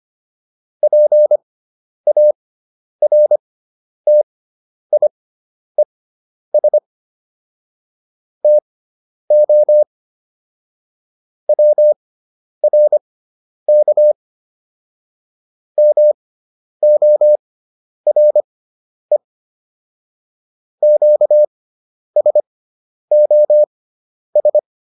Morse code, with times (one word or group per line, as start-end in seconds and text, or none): 0.8-6.8 PARTIES
8.4-9.8 TO
11.5-14.1 WRK
15.8-19.2 MORE
20.8-24.6 QSOS